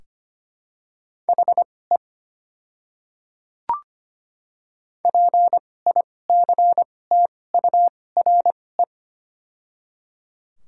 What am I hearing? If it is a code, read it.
Morse code, heard as a keyed tone, about 25 wpm, spaced as for 18 wpm, 700 Hz. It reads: HE PICTURE
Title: 2x Character Spacing - Top 200 Words - 25wpm